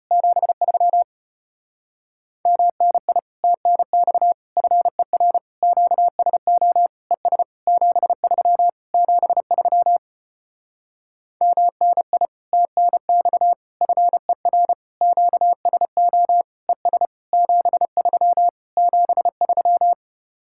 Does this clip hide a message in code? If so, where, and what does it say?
Yes: Morse code from 0.1 to 19.9 s, heard as a keyed tone, about 33 wpm, 700 Hz. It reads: 73 MNI TNX FER QSO ES 73 73 MNI TNX FER QSO ES 73 73